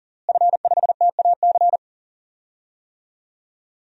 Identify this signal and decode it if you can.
Morse code, heard as a keyed tone, about 40 wpm, 700 Hz. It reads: F5TAC
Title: DXCC-100 Call Signs - 5 Characters - Repeated 2x - 40wpm